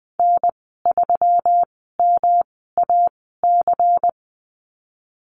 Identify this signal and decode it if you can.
Morse code, heard as a keyed tone, about 20 wpm, spaced as for 15 wpm, 700 Hz. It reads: N3MAC
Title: 161 Lesson 26g - Letter g - Call Signs 20wpm